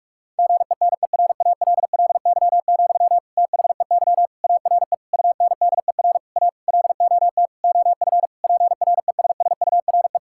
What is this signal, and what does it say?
Morse code, heard as a keyed tone, about 41 wpm, 700 Hz. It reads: GENERALLY, THEY ARE UNDER A LOT OF PRESSURE
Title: Sentences from Top 1000 Words - Review Part 6 - 45wpm